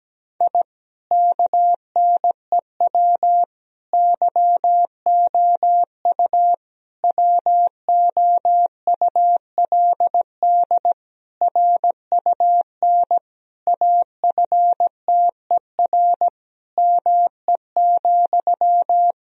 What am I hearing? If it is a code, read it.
Morse code, heard as a keyed tone, about 17 wpm, 700 Hz. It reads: I KNEW YOU WOULD RUN AFTER ME,
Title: Sentences from Top 1000 Words - Review Part 1 - 17wpm